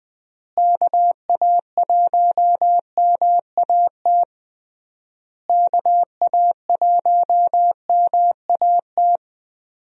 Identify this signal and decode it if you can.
Morse code, heard as a keyed tone, about 20 wpm, 700 Hz. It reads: KA1MAT KA1MAT
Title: USA Call Signs - 6 Characters - Repeated 3x - 20wpm